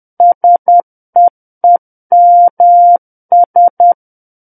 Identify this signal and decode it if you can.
Morse code, heard as a keyed tone, about 10 wpm, 700 Hz. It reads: SEEMS
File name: Book One / 01-Chapter 1 - 10 WPM